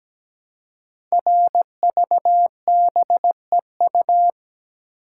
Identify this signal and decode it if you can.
Morse code, heard as a keyed tone, about 17 wpm, 700 Hz. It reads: RVBEU
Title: Code Groups - 5 Letters - 17wpm